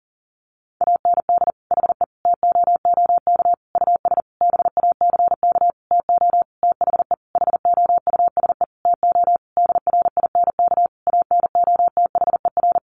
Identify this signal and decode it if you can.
Morse code, heard as a keyed tone, about 40 wpm, 700 Hz. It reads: AND HE TOOK US BACK TO THE HOUSE TO DRINK ANOTHER
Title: Sentences from Top 1500 Words 40wpm